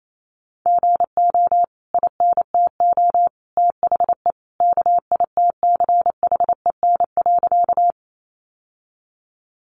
Morse code, heard as a keyed tone, about 28 wpm, 700 Hz. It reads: GO INTO THE KITCHEN.
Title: Sentences from Top 1400 Words 28wpm